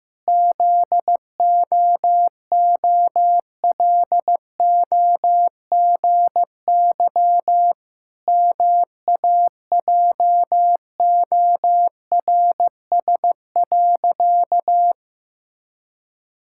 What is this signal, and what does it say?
Morse code, heard as a keyed tone, about 14 wpm, 700 Hz. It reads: ZOOLOGY MAJORS.
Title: Pangram Challenge v2 - 15wpm